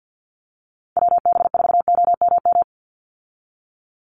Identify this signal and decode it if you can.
Morse code, heard as a keyed tone, about 50 wpm, 700 Hz. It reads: WB4OMM
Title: Call Signs Repeated 3x - 50wpm